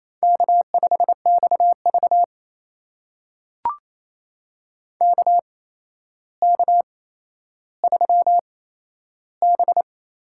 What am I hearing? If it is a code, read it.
Morse code, heard as a keyed tone, about 28 wpm, spaced as for 12 wpm, 700 Hz. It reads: K5XV KK3B
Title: USA Call Signs - 4 Characters - 8x Character Spacing - 28wpm